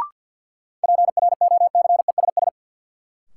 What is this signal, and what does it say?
Morse code, heard as a keyed tone, about 47 wpm, 700 Hz. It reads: PROCESS
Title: Sentences from Top 1000 Words - Review Part 3 - New Word Drill - 50wpm